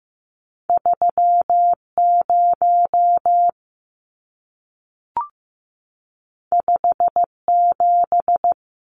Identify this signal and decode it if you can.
Morse code, heard as a keyed tone, about 15 wpm, 700 Hz. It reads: 3 0 5 7
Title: Numbers - 2-Digit 15wpm